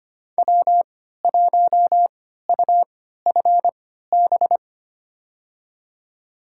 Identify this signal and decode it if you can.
Morse code, heard as a keyed tone, about 25 wpm, spaced as for 14 wpm, 700 Hz. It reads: W1UFB
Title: USA Call Signs - 5 Characters - 3x Character Spacing - 25wpm